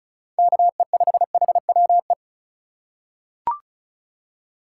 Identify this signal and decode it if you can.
Morse code, heard as a keyed tone, about 36 wpm, 700 Hz. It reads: KE5HWE
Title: USA Call Signs - 6 Characters - 3x Character Spacing - 35wpm